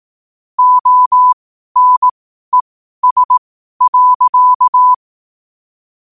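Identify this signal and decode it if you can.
Morse code, heard as a keyed tone, about 18 wpm, spaced as for 12 wpm, 1000 Hz. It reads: ONES.
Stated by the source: Morse question-of-the-day sent at 12 wpm.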